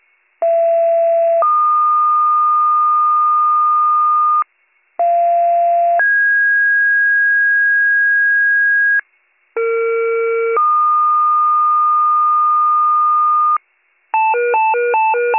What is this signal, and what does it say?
Two-tone page, heard as a one-second tone, then a three-second tone, about 670 then 1220 Hz, about 690 then 1650 Hz, about 460 then 1220 Hz. It is heard from 0.4 to 13.6 s.